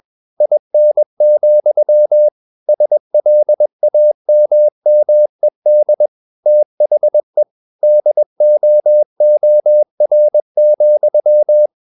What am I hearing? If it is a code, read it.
Morse code, heard as a keyed tone, about 21 wpm, 600 Hz. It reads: IN, SLAMMED THE DOOR,
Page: Romanen Never af Ken Follet på engelsk.